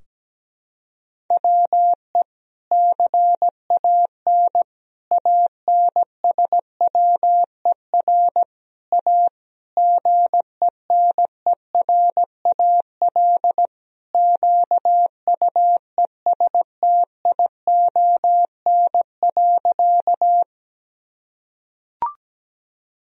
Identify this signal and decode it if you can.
Morse code, heard as a keyed tone, about 17 wpm, 700 Hz. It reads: WE CAN ANSWER A GENERAL QUESTION.
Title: Sentences from Top 1000 Words - Review Part 3 - 17wpm